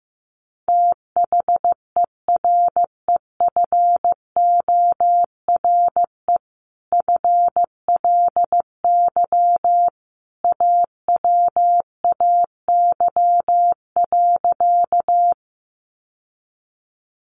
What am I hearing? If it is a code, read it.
Morse code, heard as a keyed tone, about 15 wpm, 700 Hz. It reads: THEREFORE FLY AWAY.
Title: Sentences from Top 800 Words 15wpm